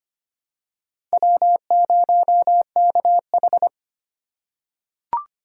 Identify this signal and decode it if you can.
Morse code, heard as a keyed tone, about 25 wpm, 700 Hz. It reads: W0KH
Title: USA Call Signs - 4 Characters - 4x Character Spacing - 25wpm